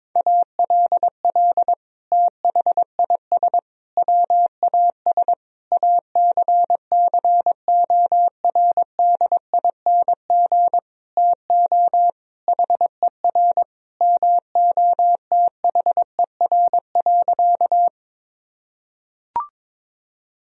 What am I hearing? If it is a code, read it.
Morse code, heard as a keyed tone, about 22 wpm, 700 Hz. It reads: ALL THIS WAS ACCORDING TO HER MOTHER.
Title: Sentences from Top 1000 Words - Review Part 4 - 22wpm